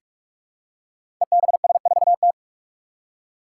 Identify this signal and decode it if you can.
Morse code, heard as a keyed tone, about 44 wpm, 700 Hz. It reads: EBS4T
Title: Five Challenging Letter-Number Combinations 45wpm